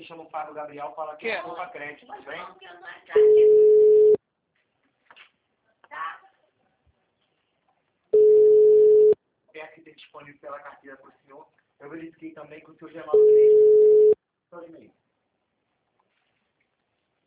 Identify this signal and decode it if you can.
Ringback tone (Europe, 425 Hz), with speech.
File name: out- -2023- - - .36789